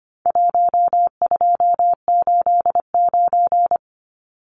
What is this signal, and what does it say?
Morse code, heard as a keyed tone, about 25 wpm, 700 Hz. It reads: 1289
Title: Numbers - Contest Style - 4-Digit 25wpm